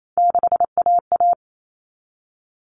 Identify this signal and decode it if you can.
Morse code, heard as a keyed tone, about 28 wpm, 700 Hz. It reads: 6AA
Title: Common Cut Numbers - 3-Digit 28wpm